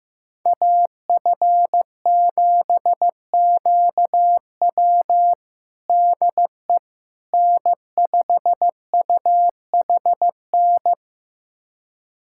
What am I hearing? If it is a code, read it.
Morse code, heard as a keyed tone, about 15 wpm, 700 Hz. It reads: AF7QW DE N5UHN
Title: USA Callsign DE Callsign - 5 Characters - 15wpm